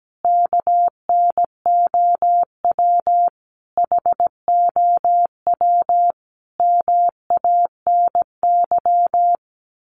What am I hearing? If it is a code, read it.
Morse code, heard as a keyed tone, about 17 wpm, 700 Hz. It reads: KNOW HOW MANY
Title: Sets of 5 Words Encore 17wpm